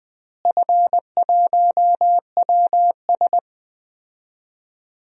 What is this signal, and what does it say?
Morse code, heard as a keyed tone, about 20 wpm, 690 Hz. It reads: F1WS
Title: DXCC-100 Call Signs - 4 Characters - Repeated 3x - 20wpm